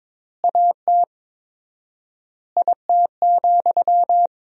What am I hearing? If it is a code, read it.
Morse code, heard as a keyed tone, about 22 wpm, 700 Hz. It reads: AT IT,
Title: 4x Word-Spacing - Sentences from Top 200 Words - 22wpm